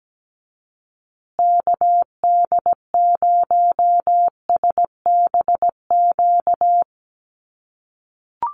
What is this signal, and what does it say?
Morse code, heard as a keyed tone, about 17 wpm, 700 Hz. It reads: KD0SBQ
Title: USA Call Signs Repeated 3x 17wpm